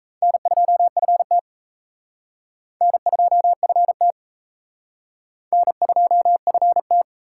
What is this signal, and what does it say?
Morse code, heard as a keyed tone, about 34 wpm, 700 Hz. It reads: N2FT N2FT N2FT
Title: Speed Racing - USA 1x2 Call Signs 28wpm